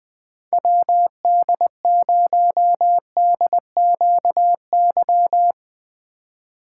Morse code, heard as a keyed tone, about 21 wpm, 700 Hz. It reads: WD0DQY
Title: Speed Racing - USA 2x3 Call Signs 17wpm